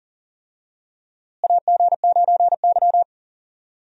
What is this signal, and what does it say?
Morse code, heard as a keyed tone, about 37 wpm, 700 Hz. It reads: AG9Y
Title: USA Call Signs - 4 Characters - Repeated 2x - 40wpm